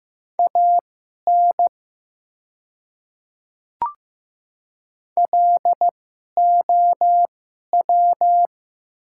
Morse code, heard as a keyed tone, about 15 wpm, spaced as for 11 wpm, 700 Hz. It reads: AN LOW
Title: 2x Character Spacing - Top 200 Words - 15wpm